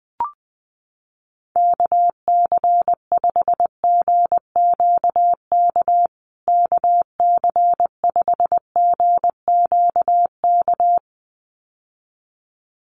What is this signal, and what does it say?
Morse code, heard as a keyed tone, about 20 wpm, 700 Hz. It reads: KC5GQK KC5GQK
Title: USA Call Signs Repeated 2x 20wpm